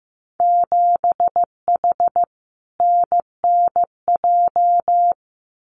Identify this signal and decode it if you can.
Morse code, heard as a keyed tone, about 15 wpm, 700 Hz. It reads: 7H NNJ
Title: Winter Field Day Exchanges Encore 15wpm